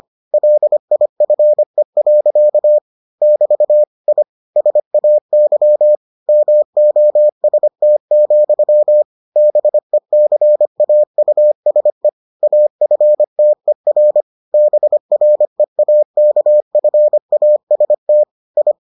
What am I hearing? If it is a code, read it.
Morse code, heard as a keyed tone, about 25 wpm, 600 Hz. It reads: LIFE. <BT> I SAY MOST, BECAUSE AFTER BREAKFAST I